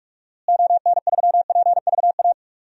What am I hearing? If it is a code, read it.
Morse code, heard as a keyed tone, about 47 wpm, 700 Hz. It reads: ON3PVA
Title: DXCC-100 Call Signs - 6 Characters - Repeated 2x - 45wpm